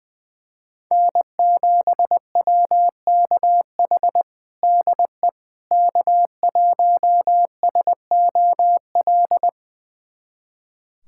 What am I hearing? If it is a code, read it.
Morse code, heard as a keyed tone, about 20 wpm, 700 Hz. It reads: N7WKH DE K1SOL
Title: USA Callsign DE Callsign - 5 Characters - 20wpm